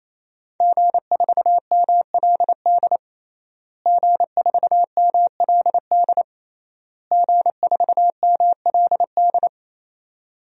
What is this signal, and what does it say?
Morse code, heard as a keyed tone, about 28 wpm, 700 Hz. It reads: G4MLD G4MLD G4MLD
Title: DXCC-100 Call Signs - 5 Characters - Repeated 3x - 28wpm